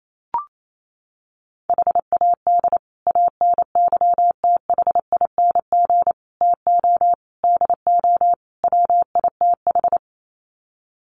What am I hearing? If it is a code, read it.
Morse code, heard as a keyed tone, about 28 wpm, 700 Hz. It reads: HAD ANYTHING TO DO WITH
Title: Sets of 5 Words Encore 28wpm